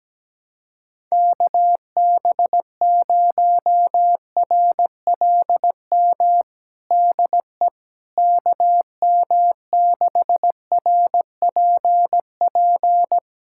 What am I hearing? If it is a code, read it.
Morse code, heard as a keyed tone, about 17 wpm, 700 Hz. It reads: KB0RLM DE KM6RPP
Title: USA Callsign DE Callsign - 6 Characters - 17wpm